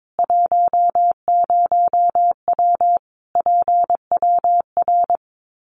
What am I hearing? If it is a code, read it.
Morse code, heard as a keyed tone, about 22 wpm, 700 Hz. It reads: 10W PWR
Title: Common QSO Phrases 22wpm